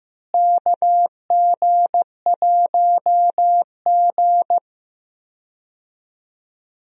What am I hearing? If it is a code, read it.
Morse code, heard as a keyed tone, about 16 wpm, 700 Hz. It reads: KG1G
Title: USA Call Signs - 4 Characters - Repeated 2x - 15wpm